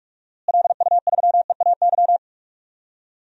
Morse code, heard as a keyed tone, about 46 wpm, 700 Hz. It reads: LU3EAY
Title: DXCC-100 Call Signs - 6 Characters - Repeated 3x - 45wpm